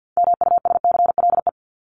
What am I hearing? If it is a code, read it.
Morse code, heard as a keyed tone, about 50 wpm, 700 Hz. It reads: MUSCLE
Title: Top 1251-1500 Words 50wpm